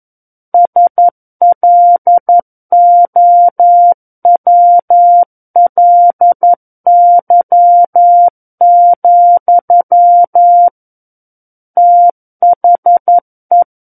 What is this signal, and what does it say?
Morse code, heard as a keyed tone, about 11 wpm, 700 Hz. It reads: SLOWLY, THE
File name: 11 WPM